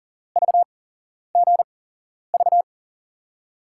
Morse code, heard as a keyed tone, about 40 wpm, spaced as for 11 wpm, 700 Hz. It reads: VGV
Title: USA Call Signs - 5 Characters - 8x Character Spacing - 40wpm